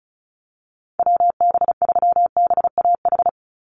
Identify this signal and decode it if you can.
Morse code, heard as a keyed tone, about 34 wpm, 700 Hz. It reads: WB3BAH
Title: USA Call Signs 35wpm